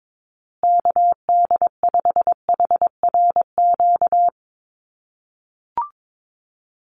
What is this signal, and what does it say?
Morse code, heard as a keyed tone, about 22 wpm, 700 Hz. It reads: KD5HRQ
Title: USA Call Signs Repeated 3x 22wpm